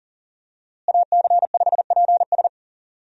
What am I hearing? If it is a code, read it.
Morse code, heard as a keyed tone, about 37 wpm, 700 Hz. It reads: AC5PS
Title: USA Call Signs - 5 Characters - Repeated 3x - 40wpm